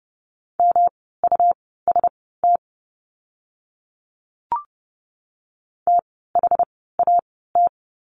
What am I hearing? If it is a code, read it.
Morse code, heard as a keyed tone, about 30 wpm, spaced as for 17 wpm, 700 Hz. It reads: MUST THAT
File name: course / 30:10wpm / 237 Lesson 38f - Letter q - Words 30wpm